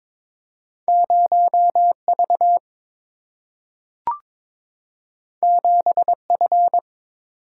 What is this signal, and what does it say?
Morse code, heard as a keyed tone, about 22 wpm, 700 Hz. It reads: 0V 7F
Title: Single-Letter Single-Number 22wpm